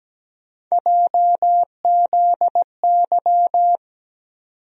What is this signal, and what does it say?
Morse code, heard as a keyed tone, about 17 wpm, 700 Hz. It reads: JZY
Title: USA Call Sign Postfixes - 3 Letters 17wpm